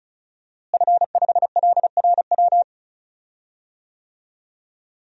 Morse code, heard as a keyed tone, about 36 wpm, 700 Hz. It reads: F5LRW
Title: DXCC-100 Call Signs - 5 Characters - Repeated 2x - 35wpm